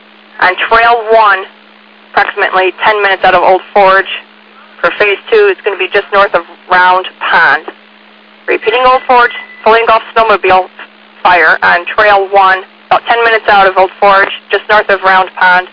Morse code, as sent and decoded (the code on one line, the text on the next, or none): none